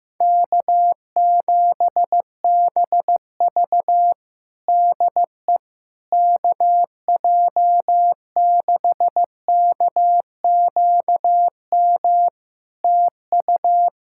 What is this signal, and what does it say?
Morse code, heard as a keyed tone, about 15 wpm, 700 Hz. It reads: K7BV DE KJ6KQM TU
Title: Practice QSO - Exchange 4 - Ending - 15wpm